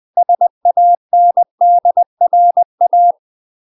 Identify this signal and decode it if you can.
Morse code, heard as a keyed tone, about 20 wpm, 700 Hz. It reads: SANDRA
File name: UK MASTER QSOs TRAINING/20WPM